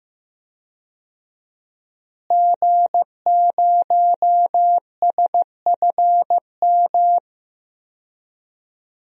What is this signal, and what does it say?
Morse code, heard as a keyed tone, about 15 wpm, 700 Hz. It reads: G0SFM
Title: DXCC-100 Call Signs - 5 Characters - Repeated 2x - 15wpm